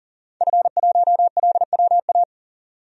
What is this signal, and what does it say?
Morse code, heard as a keyed tone, about 40 wpm, 700 Hz. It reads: F1LWA
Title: DXCC-100 Call Signs - 5 Characters - 40wpm